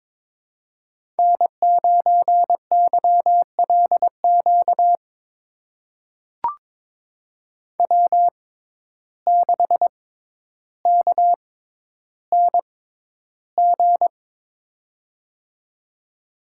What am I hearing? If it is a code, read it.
Morse code, heard as a keyed tone, about 22 wpm, spaced as for 11 wpm, 700 Hz. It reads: N9YLQ W6KNG
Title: USA Call Signs - 6x Character Spacing - 22wpm